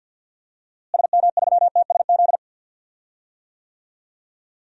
Morse code, heard as a keyed tone, about 53 wpm, 700 Hz. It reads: SM3TSZ
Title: DXCC-100 Call Signs - 6 Characters - Repeated 2x - 50wpm